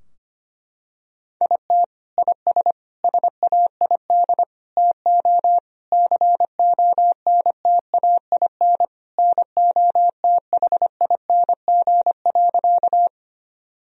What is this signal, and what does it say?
Morse code, heard as a keyed tone, about 25 wpm, 700 Hz. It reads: IT IS SAID TO CONTAIN NOTHING.
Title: Sentences from Top 1000 Words - Review Part 2 - 25wpm